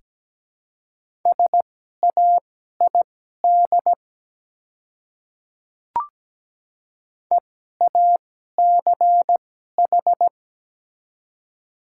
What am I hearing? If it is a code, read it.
Morse code, heard as a keyed tone, about 17 wpm, spaced as for 12 wpm, 700 Hz. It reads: SAID EACH
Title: 2x Character Spacing - Top 200 Words - 17wpm